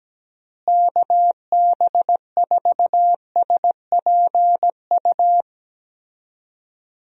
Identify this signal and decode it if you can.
Morse code, heard as a keyed tone, about 17 wpm, 700 Hz. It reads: KB4SPU
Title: USA Call Signs - 6 Characters - Repeated 3x - 17wpm